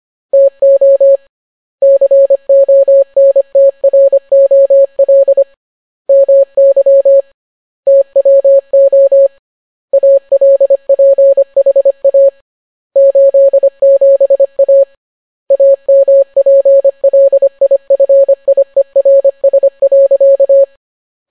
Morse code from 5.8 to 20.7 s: MY TWO ALPHA 87A AMPLIFIERS.